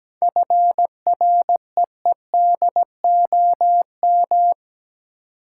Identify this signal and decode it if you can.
Morse code, heard as a keyed tone, about 17 wpm, 700 Hz. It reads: FREEDOM
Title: Seven Letter Words - 17wpm